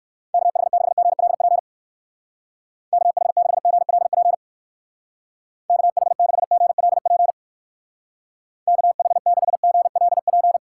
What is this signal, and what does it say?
Morse code, heard as a keyed tone, about 57 wpm, 700 Hz. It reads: KH6GLP KH6GLP KH6GLP KH6GLP